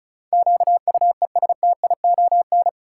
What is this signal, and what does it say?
Morse code, heard as a keyed tone, about 34 wpm, 700 Hz. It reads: QUESTION
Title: Sets of 5 Words Encore - New Word Drill - 35wpm